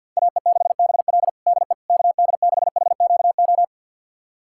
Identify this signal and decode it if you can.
Morse code, heard as a keyed tone, about 50 wpm, 700 Hz. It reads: AE6BL DE KD6HQO
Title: USA Callsign DE Callsign 50wpm